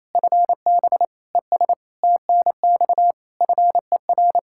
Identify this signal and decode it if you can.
Morse code, heard as a keyed tone, about 28 wpm, 700 Hz. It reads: FB ES TNX FER
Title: Practice QSO - Exchange 2 - Info - 28wpm